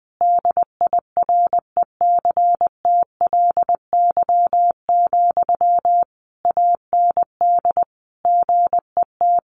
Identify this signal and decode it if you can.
Morse code, heard as a keyed tone, about 20 wpm, 700 Hz. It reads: DIRECTLY, AND GET
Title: Sentences from Top 800 Words 20wpm